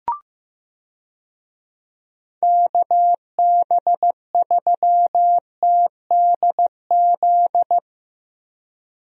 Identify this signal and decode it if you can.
Morse code, heard as a keyed tone, about 16 wpm, 700 Hz. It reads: KB3TDZ
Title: USA Call Signs - 6 Characters - 15wpm